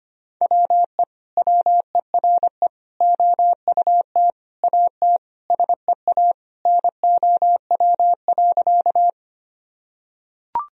Morse code, heard as a keyed tone, about 25 wpm, 700 Hz. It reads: WE WERE OUT AT SEA NOW.
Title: Sentences from Top 1000 Words - Review Part 1 - 25wpm